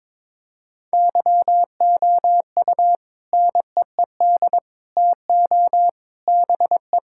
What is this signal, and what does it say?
Morse code, heard as a keyed tone, about 22 wpm, 700 Hz. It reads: YOU NEED TO BE
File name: practice / 22wpm / Sentences with 3-Word Phrases - 22wpm